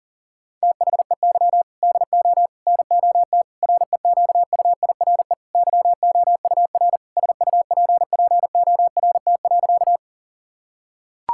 Morse code, heard as a keyed tone, about 40 wpm, 690 Hz. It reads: THEY DO NOT REQUIRE YOUR SUPPORT.
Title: Sentences from Top 1000 Words - Review Part 6 - 40wpm